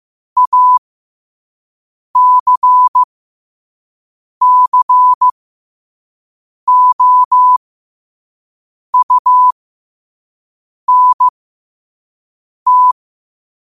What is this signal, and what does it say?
Morse code, heard as a keyed tone, about 15 wpm, spaced as for 5 wpm, 1000 Hz. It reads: ACCOUNT
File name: qotdfiles / 5wpm